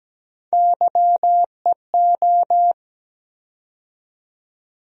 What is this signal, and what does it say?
Morse code, heard as a keyed tone, about 17 wpm, 700 Hz. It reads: YEO
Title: USA Call Sign Postfixes - 3 Letters 17wpm